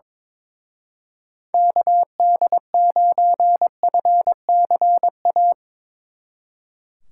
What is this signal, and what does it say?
Morse code, heard as a keyed tone, about 22 wpm, 700 Hz. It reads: KD9FCA
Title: Speed Racing - USA Call Signs 22wpm